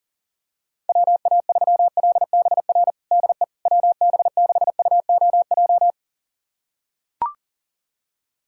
Morse code, heard as a keyed tone, about 40 wpm, 700 Hz. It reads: WA3LBR DE WB6UOJ
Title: USA Callsign DE Callsign - 6 Characters - 40wpm